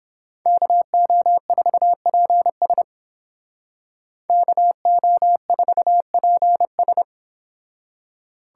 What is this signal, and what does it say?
Morse code, heard as a keyed tone, about 29 wpm, 700 Hz. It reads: KO4PS KO4PS